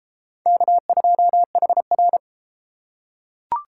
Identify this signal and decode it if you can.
Morse code, heard as a keyed tone, about 31 wpm, 700 Hz. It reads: K2HR